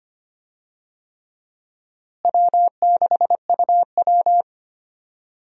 Morse code, heard as a keyed tone, about 25 wpm, 700 Hz. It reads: W6UW
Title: USA Call Signs - 4 Characters - Repeated 3x - 25wpm